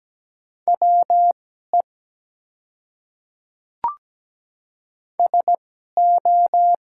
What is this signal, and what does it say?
Morse code, heard as a keyed tone, about 17 wpm, spaced as for 12 wpm, 700 Hz. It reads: WE SO
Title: 2x Character Spacing - Top 100 Words - 17wpm